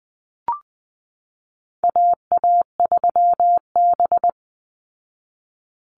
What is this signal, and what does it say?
Morse code, heard as a keyed tone, about 20 wpm, 700 Hz. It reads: AA3B
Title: Call Signs Repeated 3x - 20wpm